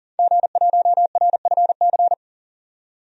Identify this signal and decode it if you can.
Morse code, heard as a keyed tone, about 40 wpm, 700 Hz. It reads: G1RFC
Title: DXCC-100 Call Signs - 5 Characters - 40wpm